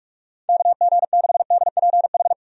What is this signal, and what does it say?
Morse code, heard as a keyed tone, about 46 wpm, 700 Hz. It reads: KG6DPH
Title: USA Call Signs - 6 Characters - 45wpm